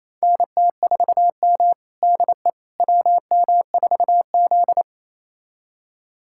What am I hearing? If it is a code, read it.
Morse code, heard as a keyed tone, about 28 wpm, 700 Hz. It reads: NT4M DE WM4Z
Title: USA Callsign DE Callsign - 4 Characters - 28wpm